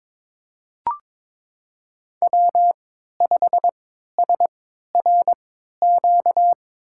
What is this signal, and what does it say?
Morse code, heard as a keyed tone, about 23 wpm, spaced as for 13 wpm, 700 Hz. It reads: W5SRQ
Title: USA Call Signs - 5 Characters - 3x Character Spacing - 22wpm